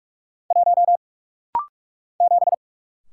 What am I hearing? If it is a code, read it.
Morse code, heard as a keyed tone, about 43 wpm, 700 Hz. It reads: 1 7
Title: Single Letter-Number - Rapid-Fire 45wpm